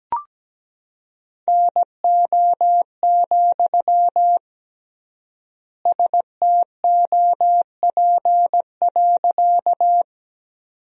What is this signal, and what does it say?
Morse code, heard as a keyed tone, about 17 wpm, 700 Hz. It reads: NO, STOP.
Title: 3x Word-Spacing - Sentences from Top 300 Words - 17wpm